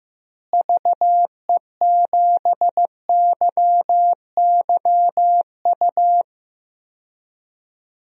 Morse code, heard as a keyed tone, about 15 wpm, 700 Hz. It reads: VE7YYU
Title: DXCC-100 Call Signs - 6 Characters - Repeated 2x - 15wpm